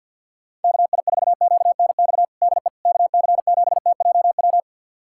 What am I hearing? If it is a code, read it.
Morse code, heard as a keyed tone, about 47 wpm, 700 Hz. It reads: KI4QNX DE KC7TJW
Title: USA Callsign DE Callsign - 6 Characters - 50wpm